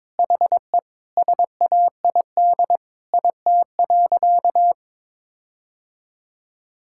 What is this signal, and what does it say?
Morse code, heard as a keyed tone, about 22 wpm, 700 Hz. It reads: HE SAID IT.
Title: Sentences from Top 1600 Words 22wpm